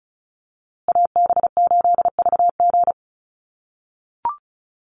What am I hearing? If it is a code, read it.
Morse code, heard as a keyed tone, about 34 wpm, 700 Hz. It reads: AB8VG
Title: USA Call Signs Repeated 3x 35wpm